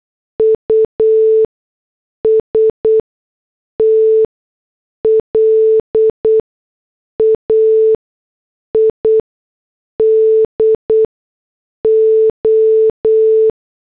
Morse code, heard as a keyed tone, about 8 wpm, spaced as for 6 wpm, 440 Hz. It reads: UST LAID O